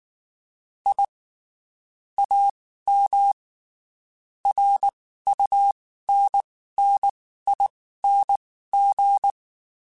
Morse code, heard as a keyed tone, about 19 wpm, spaced as for 14 wpm, 800 Hz. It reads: I AM RUNNING
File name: (spaced) / QSOs Spaced-mp3s / WA7QOX 2 @18 sp